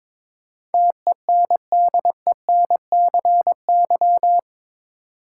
Morse code, heard as a keyed tone, about 22 wpm, 700 Hz. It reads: TENDENCY
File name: practice / 22wpm / Eight Letter Words - 22wpm